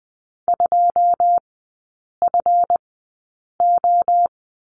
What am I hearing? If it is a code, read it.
Morse code, heard as a keyed tone, about 20 wpm, spaced as for 8 wpm, 700 Hz. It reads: 2FO
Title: 218 Lesson 35e - Number 0 - Three Characters 20wpm